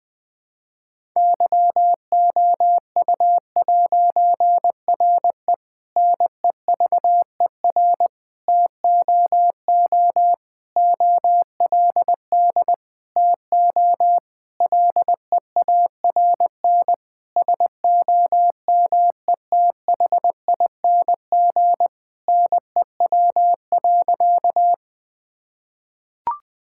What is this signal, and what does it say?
Morse code, heard as a keyed tone, about 21 wpm, 700 Hz. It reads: YOU'RE NEVER TOO OLD TO LEARN SOMETHING NEW.
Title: Difficult Sentences with Hints - 20wpm